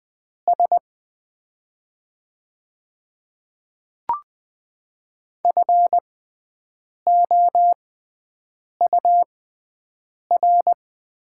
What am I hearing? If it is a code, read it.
Morse code, heard as a keyed tone, about 20 wpm, spaced as for 7 wpm, 700 Hz. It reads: S FOUR
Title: 6x Character Spacing - Top 300 Words - 20wpm